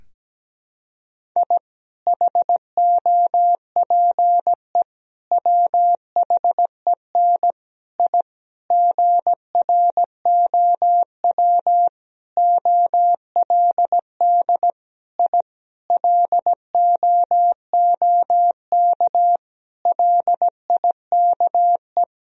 Morse code, heard as a keyed tone, about 17 wpm, 700 Hz. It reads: I HOPE WHEN I GROW OLD I LOOK LIKE
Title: Sentences from Top 1000 Words - Review Part 1 - 17wpm